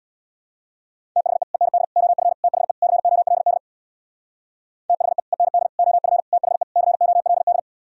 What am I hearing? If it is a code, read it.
Morse code, heard as a keyed tone, about 64 wpm, 700 Hz. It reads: THE END OF THE COLD THE END OF THE COLD